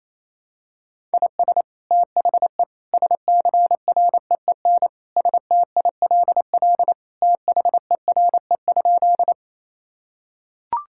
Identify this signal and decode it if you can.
Morse code, heard as a keyed tone, about 28 wpm, 700 Hz. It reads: IS THE SCREEN STILL THERE?
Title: Sentences from Top 1000 Words - Review Part 6 - 28wpm